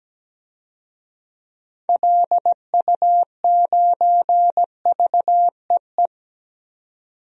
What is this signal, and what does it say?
Morse code, heard as a keyed tone, about 17 wpm, 700 Hz. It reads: LU9VEE
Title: DXCC-100 Call Signs - 6 Characters - 17wpm